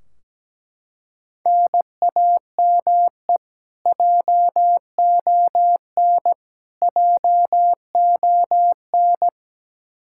Morse code, heard as a keyed tone, about 17 wpm, 700 Hz. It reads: NAME JON JON
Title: K1USN SST QSO Practice 17wpm